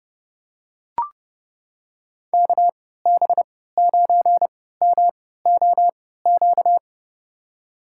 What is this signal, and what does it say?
Morse code, heard as a keyed tone, about 30 wpm, spaced as for 17 wpm, 700 Hz. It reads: KB9MOQ
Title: USA Call Signs - 6 Characters - 3x Character Spacing - 30wpm